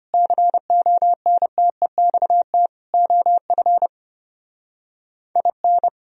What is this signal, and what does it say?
Morse code, heard as a keyed tone, about 31 wpm, 700 Hz. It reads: CONTEXT OF IN
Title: Speed-Racing - Sets of 4 Words 25wpm